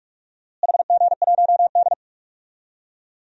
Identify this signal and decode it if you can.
Morse code, heard as a keyed tone, about 43 wpm, 700 Hz. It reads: HG1D
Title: DXCC-100 Call Signs - 4 Characters - Repeated 2x - 45wpm